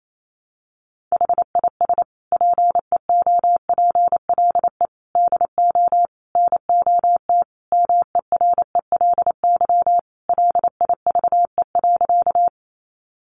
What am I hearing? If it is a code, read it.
Morse code, heard as a keyed tone, about 28 wpm, 700 Hz. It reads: HIS PEOPLE DO NOT MERELY LIVE.
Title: Sentences from Top 1400 Words 28wpm